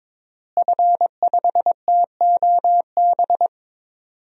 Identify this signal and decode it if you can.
Morse code, heard as a keyed tone, about 22 wpm, 700 Hz. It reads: F5TOB
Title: DXCC-100 Call Signs - 5 Characters - Repeated 3x - 22wpm